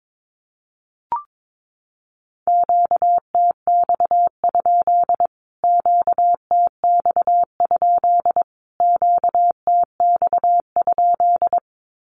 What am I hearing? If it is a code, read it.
Morse code, heard as a keyed tone, about 22 wpm, 700 Hz. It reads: QTX? QTX? QTX?
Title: Q-Codes Repeated 3x 22wpm